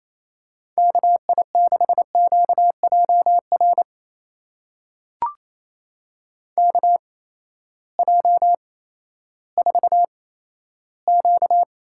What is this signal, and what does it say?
Morse code, heard as a keyed tone, about 27 wpm, spaced as for 14 wpm, 700 Hz. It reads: KI6QJR KJ4Q
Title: USA Call Signs - 6 Characters - 8x Character Spacing - 28wpm